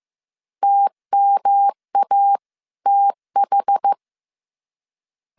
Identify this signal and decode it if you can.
Morse code, heard as a keyed tone, about 15 wpm, 800 Hz. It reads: TMATH